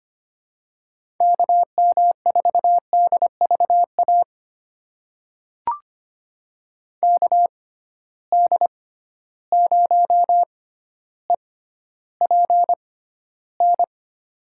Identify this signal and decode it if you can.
Morse code, heard as a keyed tone, about 25 wpm, spaced as for 13 wpm, 700 Hz. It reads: KM4DVA KD0EPN
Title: USA Call Signs - 6 Characters - 6x Character Spacing - 25wpm